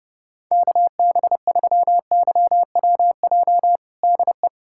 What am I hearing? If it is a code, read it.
Morse code, heard as a keyed tone, about 30 wpm, 700 Hz. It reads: KB3YWJ DE
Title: USA Callsign DE Callsign 30wpm